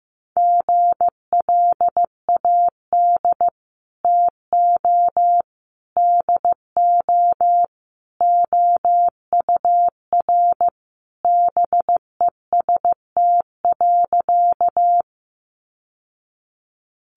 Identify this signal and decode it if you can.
Morse code, heard as a keyed tone, about 15 wpm, 700 Hz. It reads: GLAD TO DO OUR BEST.
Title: Sentences from Top 800 Words 15wpm